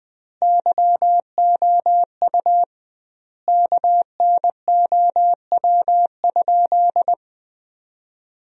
Morse code, heard as a keyed tone, about 20 wpm, 700 Hz. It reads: YOU KNOW?
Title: 2x Word-Spacing - Sentences from Top 100 Words - 20wpm